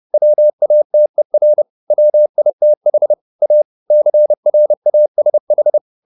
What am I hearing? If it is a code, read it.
Morse code, heard as a keyed tone, about 30 wpm, 600 Hz. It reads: WATER WITH A CRASH